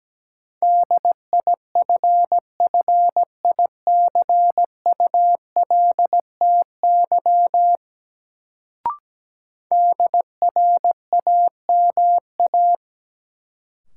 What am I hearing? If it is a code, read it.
Morse code, heard as a keyed tone, about 17 wpm, 700 Hz. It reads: DIFFICULTY DRAMA
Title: Sentences from Top 1500 Words - New Word Drill - 17wpm